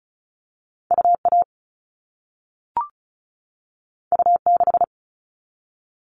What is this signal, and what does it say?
Morse code, heard as a keyed tone, about 36 wpm, 700 Hz. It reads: UA U6
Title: All Cut Numbers - 2-Digit 35wpm